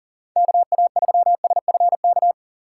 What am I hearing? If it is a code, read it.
Morse code, heard as a keyed tone, about 41 wpm, 700 Hz. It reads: KA3SFK
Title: USA Call Signs - 6 Characters - 6x Character Spacing - 40wpm